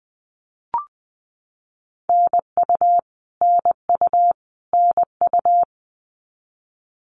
Morse code, heard as a keyed tone, about 20 wpm, 700 Hz. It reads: NU NU NU
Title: Canadian Provinces and Territories - Repeated 3x - 20wpm